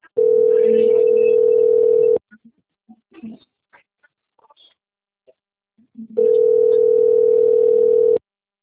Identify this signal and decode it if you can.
Ringback tone (North America), 440 + 480 Hz.
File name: out- - - - - .2747